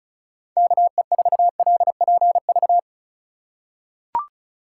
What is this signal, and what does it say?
Morse code, heard as a keyed tone, about 36 wpm, 700 Hz. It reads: KE4LPV
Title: USA Call Signs - 6 Characters - 3x Character Spacing - 35wpm